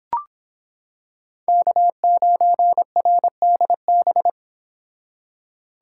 Morse code, heard as a keyed tone, about 25 wpm, 700 Hz. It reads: K9RDB